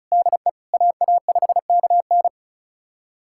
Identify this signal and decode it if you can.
Morse code, heard as a keyed tone, about 34 wpm, 700 Hz. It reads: DE AA5KN
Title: USA Callsign DE Callsign - 5 Characters - 35wpm